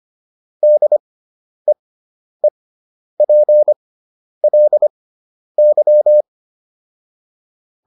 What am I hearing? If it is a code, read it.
Morse code, heard as a keyed tone, about 25 wpm, spaced as for 10 wpm, 600 Hz. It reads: DEEPLY